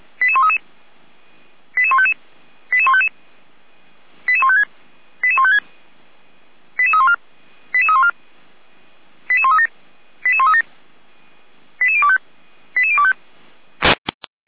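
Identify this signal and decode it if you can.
Five-tone selective call (ZVEI), heard as quick runs of five tones, tones of about 70 ms. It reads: 88133 88155 88155 88156 88156 88314 88314 88137 88137 88025 88025
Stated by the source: Five-tone ZVEI selective call received by SDR, archived by the Orlamünde fire brigade (feuerwehr-orlamuende.org).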